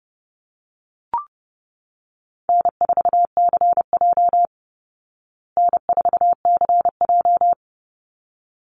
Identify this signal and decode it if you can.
Morse code, heard as a keyed tone, about 30 wpm, 700 Hz. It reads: N4CJ N4CJ
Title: Easy Call Signs Repeated 2x - 30wpm